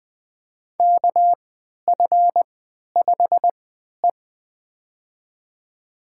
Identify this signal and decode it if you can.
Morse code, heard as a keyed tone, about 20 wpm, spaced as for 11 wpm, 700 Hz. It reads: KF5E
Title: USA Call Signs - 4 Characters - 3x Character Spacing - 20wpm